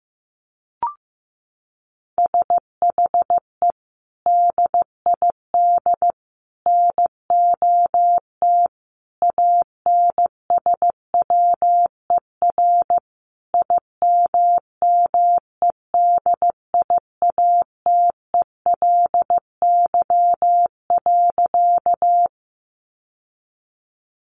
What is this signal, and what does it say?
Morse code, heard as a keyed tone, about 15 wpm, 700 Hz. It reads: SHE DID NOT ANSWER IMMEDIATELY.
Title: Sentences from Top 1100 Words 15wpm